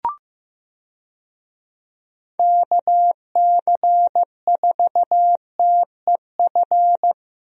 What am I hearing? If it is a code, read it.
Morse code, heard as a keyed tone, about 15 wpm, 700 Hz. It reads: KC4TEF
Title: USA Call Signs - 6 Characters - Repeated 2x - 15wpm